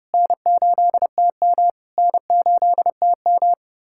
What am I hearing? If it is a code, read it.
Morse code, heard as a keyed tone, about 30 wpm, 690 Hz. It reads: N8TM N8TM
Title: SOTA QSO Practice 30wpm